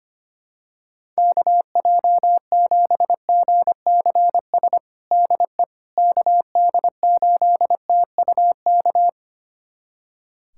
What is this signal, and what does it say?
Morse code, heard as a keyed tone, about 25 wpm, 700 Hz. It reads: KJ7GCS DE KD8TUK
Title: USA Callsign DE Callsign 25wpm